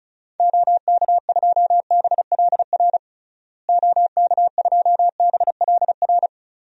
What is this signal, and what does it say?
Morse code, heard as a keyed tone, about 36 wpm, 700 Hz. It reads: OK2BLR OK2BLR
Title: DXCC-100 Call Signs - 6 Characters - Repeated 3x - 35wpm